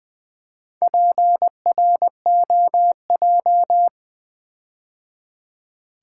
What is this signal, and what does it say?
Morse code, heard as a keyed tone, about 20 wpm, 700 Hz. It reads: PROJ
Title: Build A Word - 4 Letters At A Time - 10-Letter Words - 20wpm